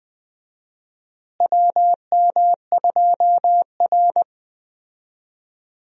Morse code, heard as a keyed tone, about 20 wpm, 700 Hz. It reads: WM2R
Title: USA Call Signs - 4 Characters - Repeated 3x - 20wpm